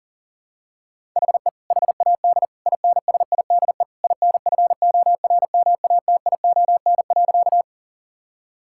Morse code, heard as a keyed tone, about 40 wpm, 700 Hz. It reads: HE HAD INSIDE INFORMATION.
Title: Sentences from Top 1000 Words - Review Part 6 - 40wpm